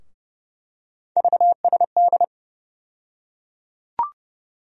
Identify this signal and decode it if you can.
Morse code, heard as a keyed tone, about 30 wpm, 700 Hz. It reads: VSD
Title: USA Call Sign Postfixes - 3 Letters 30wpm